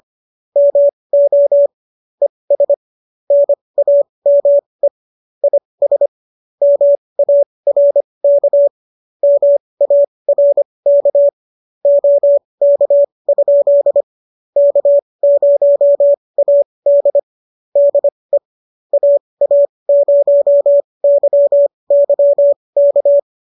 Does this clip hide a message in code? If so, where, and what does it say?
Yes: Morse code from 0.6 to 23.2 s, heard as a keyed tone, about 25 wpm, spaced as for 20 wpm, 570 Hz. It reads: MO ES NAME IS MARK MARK OK? K0AD DE AA0YY K